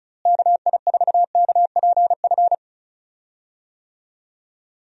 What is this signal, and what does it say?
Morse code, heard as a keyed tone, about 36 wpm, 700 Hz. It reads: KI4KPF
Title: USA Call Signs - 6 Characters - Repeated 3x - 35wpm